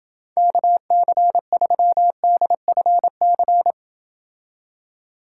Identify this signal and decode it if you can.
Morse code, heard as a keyed tone, about 29 wpm, 700 Hz. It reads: KC3DFC